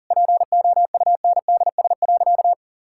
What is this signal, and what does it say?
Morse code, heard as a keyed tone, about 37 wpm, 700 Hz. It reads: POUNDS.
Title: Sentences from Top 1000 Words - Review Part 6 - 40wpm